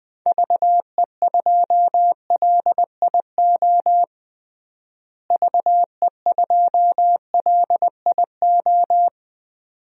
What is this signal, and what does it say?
Morse code, heard as a keyed tone, about 21 wpm, 700 Hz. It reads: VE2LIO VE2LIO
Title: DXCC-100 Call Signs - 6 Characters - Repeated 2x - 20wpm